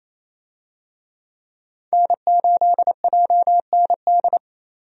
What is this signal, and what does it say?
Morse code, heard as a keyed tone, about 28 wpm, 700 Hz. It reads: N8JND
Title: USA Call Signs - 5 Characters - 28wpm